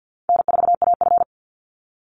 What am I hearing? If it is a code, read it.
Morse code, heard as a keyed tone, about 53 wpm, 700 Hz. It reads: N4AF
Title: Call Signs - 50wpm